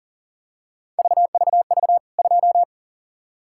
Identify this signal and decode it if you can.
Morse code, heard as a keyed tone, about 39 wpm, 700 Hz. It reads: VVV 2
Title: K1USN SST QSO Practice 40wpm